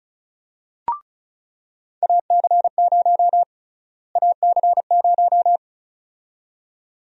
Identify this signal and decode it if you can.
Morse code, heard as a keyed tone, about 36 wpm, 700 Hz. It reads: AC0 AC0
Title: USA Call Sign Prefixes - Repeated 2x - 35wpm